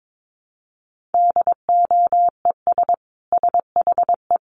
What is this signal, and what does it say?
Morse code, heard as a keyed tone, about 22 wpm, 700 Hz. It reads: DOES SHE
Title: Sentences from Top 1100 Words 22wpm